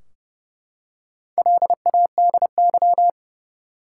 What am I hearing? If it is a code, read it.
Morse code, heard as a keyed tone, about 29 wpm, 700 Hz. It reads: LADY
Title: Four Letter Words - 30wpm